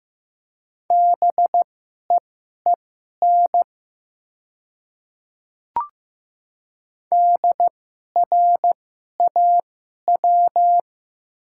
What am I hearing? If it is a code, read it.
Morse code, heard as a keyed tone, about 15 wpm, spaced as for 11 wpm, 700 Hz. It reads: BEEN DRAW
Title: 2x Character Spacing - Top 300 Words - 15wpm